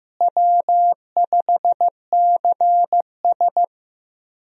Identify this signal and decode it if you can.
Morse code, heard as a keyed tone, about 15 wpm, 700 Hz. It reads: W5CS
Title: USA Call Signs - 4 Characters - Repeated 2x - 15wpm